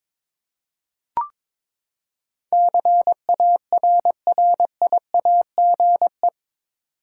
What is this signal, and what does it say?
Morse code, heard as a keyed tone, about 22 wpm, 700 Hz. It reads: CARRIAGE
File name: practice / 22wpm / Eight Letter Words - 22wpm